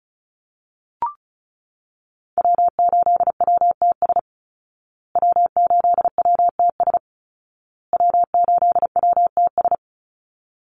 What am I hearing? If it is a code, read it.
Morse code, heard as a keyed tone, about 34 wpm, 700 Hz. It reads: W8WTS W8WTS W8WTS
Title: Call Signs Repeated 2x - 35wpm